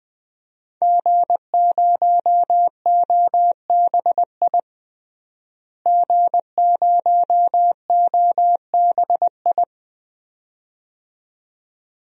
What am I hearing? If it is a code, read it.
Morse code, heard as a keyed tone, about 20 wpm, 700 Hz. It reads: G0OBI G0OBI
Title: DXCC-100 Call Signs - 5 Characters - Repeated 2x - 20wpm